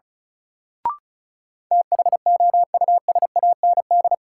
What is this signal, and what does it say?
Morse code, heard as a keyed tone, about 36 wpm, 700 Hz. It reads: THOUSAND
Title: Sentences from Top 600 Words - New Word Drill - 35wpm